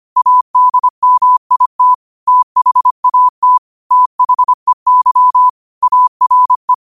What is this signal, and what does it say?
Morse code, heard as a keyed tone, about 25 wpm, 1000 Hz. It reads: ADMIT THAT THEY ARE
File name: qotdfiles / 25wpm